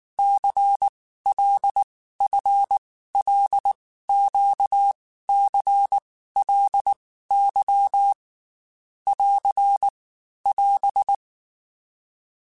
Morse code, heard as a keyed tone, about 19 wpm, 800 Hz. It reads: C L F L Q C L Y <AR> <AS>
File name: Groups@18 / Groups@18 mp3 / LFCYQ@18